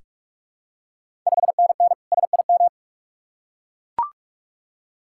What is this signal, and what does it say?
Morse code, heard as a keyed tone, about 44 wpm, 690 Hz. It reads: 5NN SIM
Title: Ontario QSO Party Exchange 45wpm